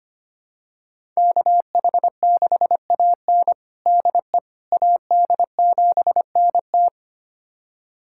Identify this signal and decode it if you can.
Morse code, heard as a keyed tone, about 25 wpm, 700 Hz. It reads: KH6AN DE AD7NT
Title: USA Callsign DE Callsign 25wpm